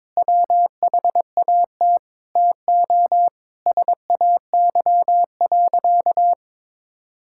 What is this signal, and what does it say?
Morse code, heard as a keyed tone, about 21 wpm, 700 Hz. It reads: WHAT TO SAY.
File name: practice / 22wpm / Sentences with 3-Word Phrases - 22wpm